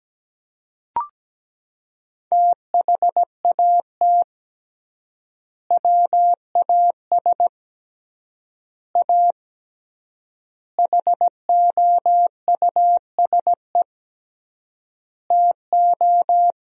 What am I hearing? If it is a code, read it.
Morse code, heard as a keyed tone, about 17 wpm, 700 Hz. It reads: THAT WAS A HOUSE TO
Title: 3x Word-Spacing - Sentences from Top 200 Words - 17wpm